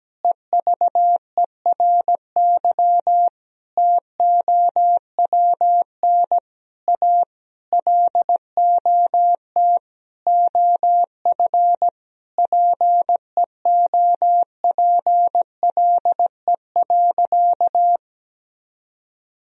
Morse code, heard as a keyed tone, about 17 wpm, 700 Hz. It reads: EVERY TOWN A LOT OF PEOPLE.
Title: Sentences from Top 1000 Words - Review Part 3 - 17wpm